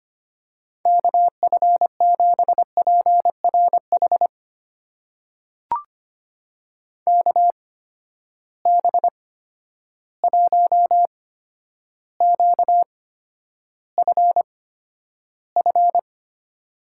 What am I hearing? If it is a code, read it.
Morse code, heard as a keyed tone, about 25 wpm, spaced as for 11 wpm, 700 Hz. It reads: KF7PRH KB1QFF
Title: USA Call Signs - 6 Characters - 8x Character Spacing - 25wpm